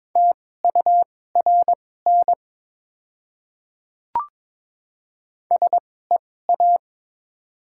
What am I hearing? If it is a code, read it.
Morse code, heard as a keyed tone, about 22 wpm, spaced as for 16 wpm, 700 Hz. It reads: TURN SEA
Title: 2x Character Spacing - Top 500 Words - 22wpm